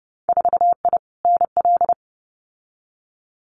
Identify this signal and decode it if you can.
Morse code, heard as a keyed tone, about 30 wpm, 700 Hz. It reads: 4I NL
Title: Winter Field Day Exchanges Encore 30wpm